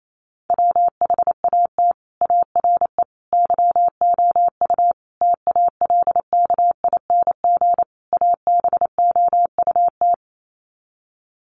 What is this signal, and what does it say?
Morse code, heard as a keyed tone, about 28 wpm, 700 Hz. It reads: WHAT ARE YOU TALKING ABOUT
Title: Sets of 5 Words 28wpm